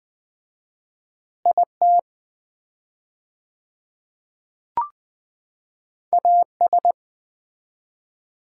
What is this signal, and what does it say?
Morse code, heard as a keyed tone, about 20 wpm, 700 Hz. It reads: IT AS
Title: 4x Word-Spacing - Sets of 5 Words - 20wpm